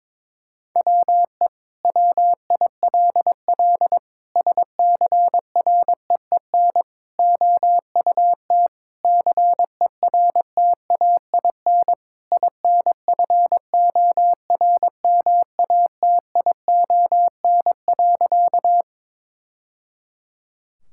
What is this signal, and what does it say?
Morse code, heard as a keyed tone, about 22 wpm, 700 Hz. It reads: WE WILL SCREEN OUT CERTAIN INFORMATION.
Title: Sentences from Top 1000 Words - Review Part 6 - 22wpm